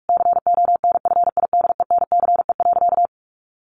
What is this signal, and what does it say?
Morse code, heard as a keyed tone, about 44 wpm, 700 Hz. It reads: CONFIDENCE.
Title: Sentences from Top 1500 Words 45wpm